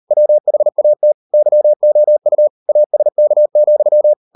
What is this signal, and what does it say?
Morse code, heard as a keyed tone, about 38 wpm, 600 Hz. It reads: WHAT YOU ASK,